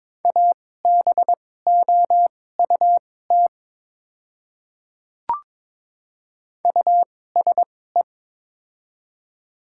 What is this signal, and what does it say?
Morse code, heard as a keyed tone, about 22 wpm, spaced as for 16 wpm, 690 Hz. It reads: ABOUT USE
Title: 2x Character Spacing - Top 100 Words - 22wpm